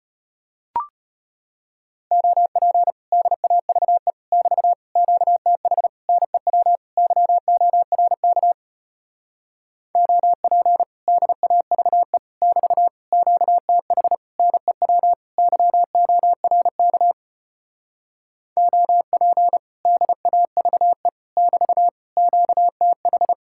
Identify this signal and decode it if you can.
Morse code, heard as a keyed tone, about 33 wpm, 700 Hz. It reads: OP DAVE <BT> QTH NEW YORK OP DAVE <BT> QTH NEW YORK OP DAVE <BT> QTH